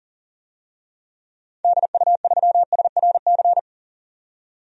Morse code, heard as a keyed tone, about 37 wpm, 700 Hz. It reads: DU3SRC
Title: DXCC-100 Call Signs - 6 Characters - 40wpm